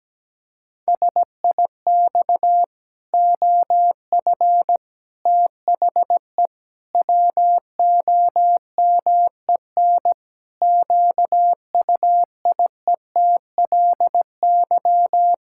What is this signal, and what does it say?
Morse code, heard as a keyed tone, about 17 wpm, 700 Hz. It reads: SIX OF THE WOMEN QUIETLY
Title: Pangram Challenge v2 - 17wpm